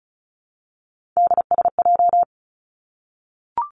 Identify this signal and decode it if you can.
Morse code, heard as a keyed tone, about 36 wpm, 700 Hz. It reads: DSJ
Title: Three Challenging Letter-Number Combinations 35wpm